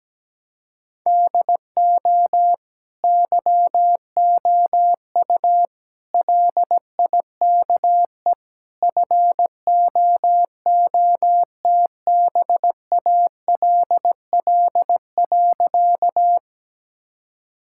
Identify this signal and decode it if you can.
Morse code, heard as a keyed tone, about 17 wpm, 700 Hz. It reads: DO YOU LIKE FOOTBALL.
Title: Sentences from Top 1600 Words 17wpm